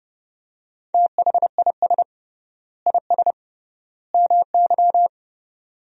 Morse code, heard as a keyed tone, about 30 wpm, 700 Hz. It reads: THIS IS MY
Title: 3x Word-Spacing - Sentences from Top 200 Words - 30wpm